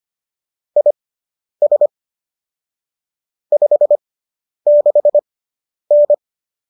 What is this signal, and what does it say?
Morse code, heard as a keyed tone, about 25 wpm, spaced as for 10 wpm, 590 Hz. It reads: IS 56N